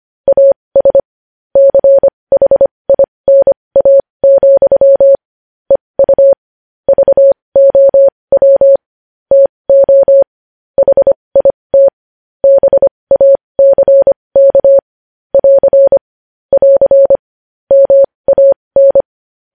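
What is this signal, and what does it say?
Morse code, heard as a keyed tone, about 25 wpm, spaced as for 20 wpm, 550 Hz. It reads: AS CHINA, EU VOW TO HIT BACK <AR> <AR> MAN
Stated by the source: Morse code news headlines practice at 20 wpm.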